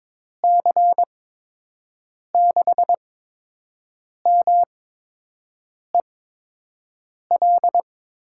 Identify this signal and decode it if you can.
Morse code, heard as a keyed tone, about 22 wpm, spaced as for 6 wpm, 700 Hz. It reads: C6MEL
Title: USA Call Signs - 6 Characters - 8x Character Spacing - 22wpm